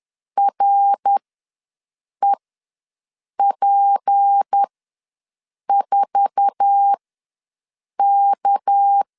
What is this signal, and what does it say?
Morse code, heard as a keyed tone, about 10 wpm, spaced as for 6 wpm, 800 Hz. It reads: REP4K
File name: LEZIONI / 8 YJ45 / 8 YJ4 ws 10wpm 4020